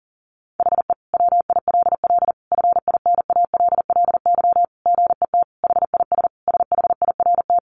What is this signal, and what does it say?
Morse code, heard as a keyed tone, about 40 wpm, 700 Hz. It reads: HE WILL FINALLY GET HIS SHIRT
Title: Additional Easy Phrases 40wpm